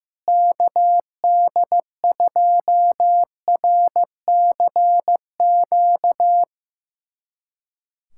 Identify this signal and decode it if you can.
Morse code, heard as a keyed tone, about 15 wpm, 700 Hz. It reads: KD2RCQ
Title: USA Callsign DE Callsign - 6 Characters - 15wpm